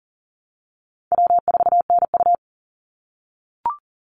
Morse code, heard as a keyed tone, about 40 wpm, 700 Hz. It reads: W4NU
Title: Easy Call Signs - 40wpm